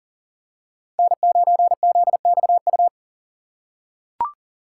Morse code, heard as a keyed tone, about 40 wpm, 700 Hz. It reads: N9ZXU
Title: USA Call Signs - 3x Character Spacing - 40wpm